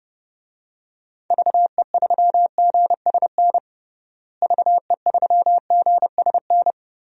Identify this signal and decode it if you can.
Morse code, heard as a keyed tone, about 32 wpm, 700 Hz. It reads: VE3GSN VE3GSN
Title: DXCC-100 Call Signs - 6 Characters - Repeated 2x - 30wpm